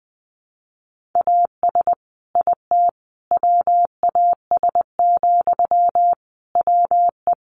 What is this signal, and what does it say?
Morse code, heard as a keyed tone, about 19 wpm, 700 Hz. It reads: AS IT WAS, WE
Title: Sentences from Top 1400 Words 20wpm